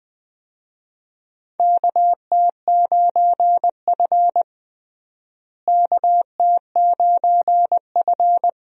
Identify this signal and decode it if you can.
Morse code, heard as a keyed tone, about 21 wpm, 700 Hz. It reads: KT9F KT9F
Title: USA Call Signs - 4 Characters - Repeated 2x - 20wpm